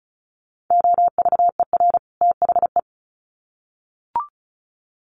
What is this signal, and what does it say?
Morse code, heard as a keyed tone, about 34 wpm, 700 Hz. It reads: OVER THE
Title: Sets of 2 Words 35wpm